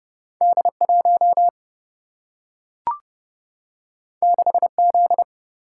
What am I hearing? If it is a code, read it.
Morse code, heard as a keyed tone, about 30 wpm, 700 Hz. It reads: D 1 6 Z
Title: Single-Letter Single-Number 30wpm